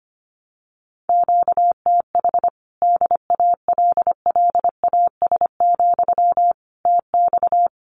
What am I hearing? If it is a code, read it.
Morse code, heard as a keyed tone, about 25 wpm, 700 Hz. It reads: QTH DALLAS, TX
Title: Common QSO Phrases Encore 25wpm